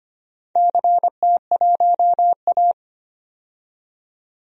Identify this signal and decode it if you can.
Morse code, heard as a keyed tone, about 25 wpm, 700 Hz. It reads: CT1A
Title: DXCC-100 Call Signs - 4 Characters - 25wpm